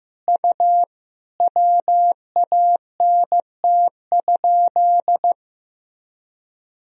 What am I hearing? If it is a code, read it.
Morse code, heard as a keyed tone, about 15 wpm, 700 Hz. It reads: U WANT?
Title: Sentences from Top 200 Words and Abbreviations 15wpm